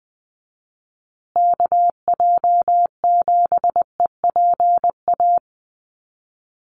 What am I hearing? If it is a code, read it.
Morse code, heard as a keyed tone, about 20 wpm, 700 Hz. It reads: KJ7EPA
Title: USA Call Signs Repeated 2x 20wpm